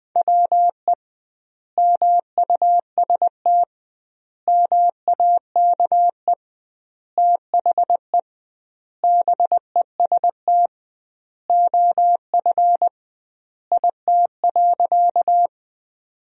Morse code, heard as a keyed tone, about 20 wpm, 700 Hz. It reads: WE MUST MAKE THE BEST OF IT.
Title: 2x Word-Spacing - Sentences from Top 400 Words - 20wpm